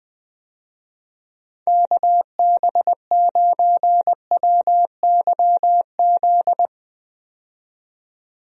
Morse code, heard as a keyed tone, about 20 wpm, 700 Hz. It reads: KB9WYZ
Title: USA Call Signs - 6 Characters - 20wpm